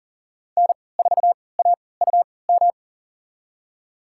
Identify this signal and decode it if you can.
Morse code, heard as a keyed tone, about 40 wpm, spaced as for 23 wpm, 700 Hz. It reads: N4AUM
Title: USA Call Signs - 5 Characters - 3x Character Spacing - 40wpm